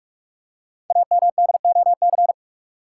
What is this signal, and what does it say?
Morse code, heard as a keyed tone, about 44 wpm, 690 Hz. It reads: AMDOC
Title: Code Groups - 5 Letters - 45wpm